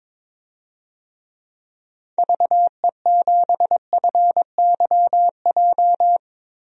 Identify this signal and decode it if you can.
Morse code, heard as a keyed tone, about 22 wpm, 700 Hz. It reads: VE7FYJ
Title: DXCC-100 Call Signs - 6 Characters - 22wpm